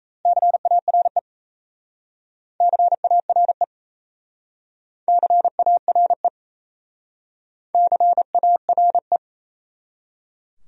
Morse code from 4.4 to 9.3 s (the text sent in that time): CARE CARE